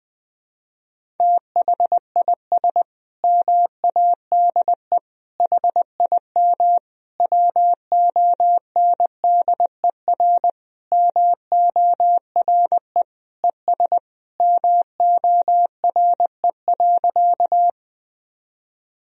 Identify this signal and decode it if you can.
Morse code, heard as a keyed tone, about 20 wpm, 700 Hz. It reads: THIS MADE HIM WONDER MORE ES MORE.
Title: Sentences from Top 500 Words and Abbreviations 20wpm